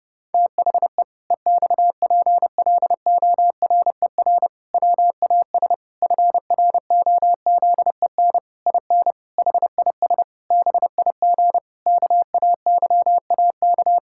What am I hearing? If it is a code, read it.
Morse code, heard as a keyed tone, about 30 wpm, 700 Hz. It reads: THE EXPLORER WAS FROZEN IN HIS BIG KAYAK
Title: Pangram Challenge v2 - 30wpm